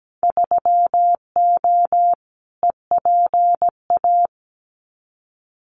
Morse code, heard as a keyed tone, about 17 wpm, 700 Hz. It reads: 3O EPA
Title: Winter Field Day Exchanges 17wpm